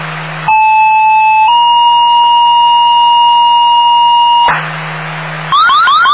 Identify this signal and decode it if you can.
Two-tone page, heard as a one-second tone, then a three-second tone, about 870 then 950 Hz, with noise. It is heard from 0.5 to 4.5 s.
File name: Indian Lake